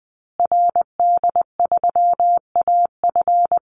Morse code, heard as a keyed tone, about 20 wpm, 700 Hz. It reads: RD3AF
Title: Call Signs Repeated 2x - 20wpm